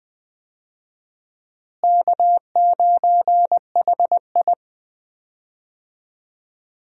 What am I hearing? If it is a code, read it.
Morse code, heard as a keyed tone, about 20 wpm, 700 Hz. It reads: K9HI
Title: USA Call Signs - 4 Characters - Repeated 2x - 20wpm